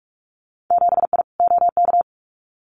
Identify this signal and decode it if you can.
Morse code, heard as a keyed tone, about 46 wpm, 700 Hz. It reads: 7I OK
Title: Winter Field Day Exchanges Encore 45wpm